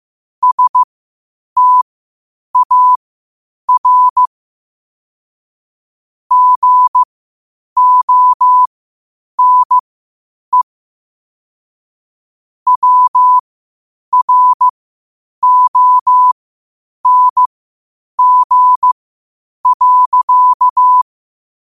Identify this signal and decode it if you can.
Morse code, heard as a keyed tone, about 15 wpm, spaced as for 8 wpm, 1000 Hz. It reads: STAR GONE WRONG.
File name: qotdfiles / 8wpm